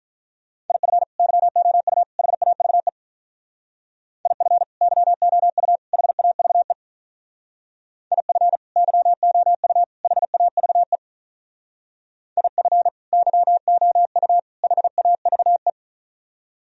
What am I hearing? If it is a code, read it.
Morse code, heard as a keyed tone, about 44 wpm, 700 Hz. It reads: IF YOU HAVE IF YOU HAVE IF YOU HAVE IF YOU HAVE